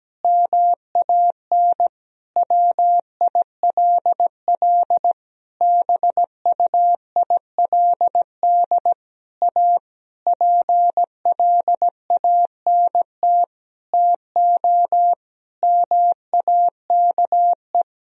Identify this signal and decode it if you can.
Morse code, heard as a keyed tone, about 17 wpm, 700 Hz. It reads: MAN WILL BUILD A PLANT TO MAKE
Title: Sentences from Top 1000 Words - Review Part 1 - 17wpm